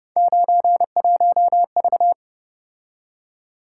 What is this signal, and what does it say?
Morse code, heard as a keyed tone, about 29 wpm, 700 Hz. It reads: 91V
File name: Speed-Racing - Three Letter-Number Combinations - 25wpm